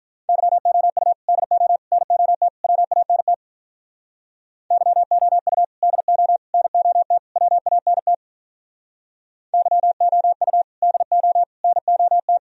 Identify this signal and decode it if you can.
Morse code, heard as a keyed tone, about 45 wpm, 700 Hz. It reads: YOU DO NOT WANT YOU DO NOT WANT YOU DO NOT